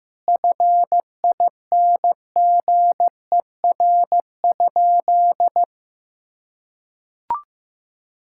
Morse code, heard as a keyed tone, about 15 wpm, 700 Hz. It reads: FINGER?
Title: Sentences from Top 1600 Words 15wpm